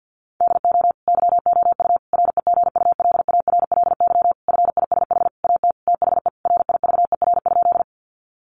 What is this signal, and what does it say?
Morse code, heard as a keyed tone, about 50 wpm, 700 Hz. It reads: DO YOU REGULARLY FISH AT THE RIVER?
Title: Sentences from Top 1400 Words 50wpm